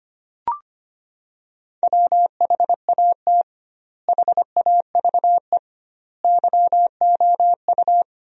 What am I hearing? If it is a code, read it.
Morse code, heard as a keyed tone, about 25 wpm, 700 Hz. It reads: WHAT HAVE YOU
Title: 2x Word-Spacing - Sentences from Top 500 Words - 25wpm